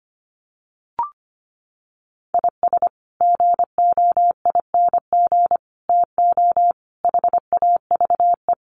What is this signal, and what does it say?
Morse code, heard as a keyed tone, about 25 wpm, 700 Hz. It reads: IS GOING TO HAVE
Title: Sets of 5 Words 25wpm